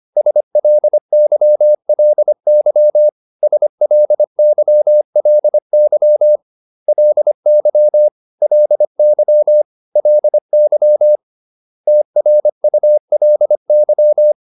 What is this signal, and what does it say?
Morse code, heard as a keyed tone, about 25 wpm, 600 Hz. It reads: SLYLY SLYLY LY LY LY TRULY